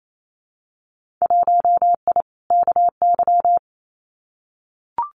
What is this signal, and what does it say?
Morse code, heard as a keyed tone, about 28 wpm, 700 Hz. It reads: 1I KY
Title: Winter Field Day Exchanges 28wpm